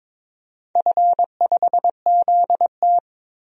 Morse code, heard as a keyed tone, about 22 wpm, 700 Hz. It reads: F5ZT
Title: DXCC-100 Call Signs - 4 Characters - 22wpm